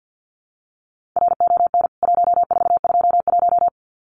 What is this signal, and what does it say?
Morse code, heard as a keyed tone, about 52 wpm, 700 Hz. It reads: RON 1421
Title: CWT Exchange 50wpm